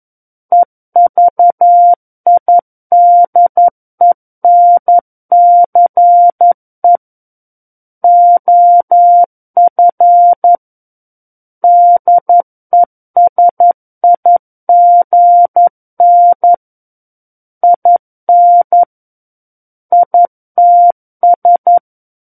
Morse code, heard as a keyed tone, about 11 wpm, 700 Hz. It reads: EVIDENCE OF DESIGN IN ITS
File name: 02-Chapter 2 - 11 WPM